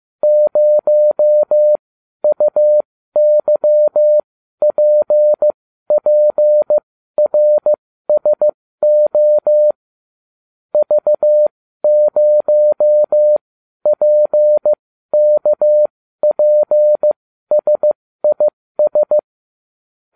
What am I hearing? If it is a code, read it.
Morse code, heard as a keyed tone, about 15 wpm, spaced as for 12 wpm, 600 Hz. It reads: 0UYPPRSO V0PKPSIS